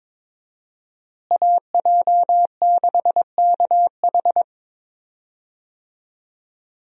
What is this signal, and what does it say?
Morse code, heard as a keyed tone, about 22 wpm, 700 Hz. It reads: AJ6KH
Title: USA Call Signs - 5 Characters - Repeated 3x - 22wpm